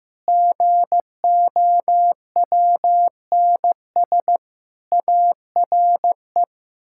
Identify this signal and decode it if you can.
Morse code, heard as a keyed tone, about 15 wpm, 700 Hz. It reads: GOWNS ARE
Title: Pangram Challenge v2 - 15wpm